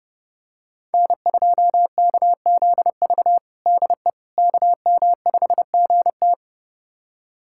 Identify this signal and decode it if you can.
Morse code, heard as a keyed tone, about 32 wpm, 700 Hz. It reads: N2KZV DE KM5GT
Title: USA Callsign DE Callsign - 5 Characters - 30wpm